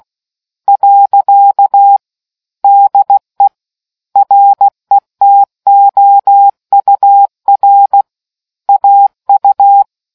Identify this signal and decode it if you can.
Morse code, heard as a keyed tone, about 16 wpm, 800 Hz. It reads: . DE RETOUR AU